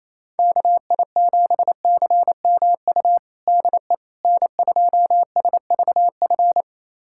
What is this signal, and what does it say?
Morse code, heard as a keyed tone, about 28 wpm, 700 Hz. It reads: KI7CMU DE N2SVF
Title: USA Callsign DE Callsign 28wpm